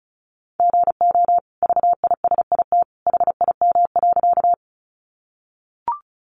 Morse code, heard as a keyed tone, about 34 wpm, 700 Hz. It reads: GO VISIT HIM.
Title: Sentences from Top 900 Words 35wpm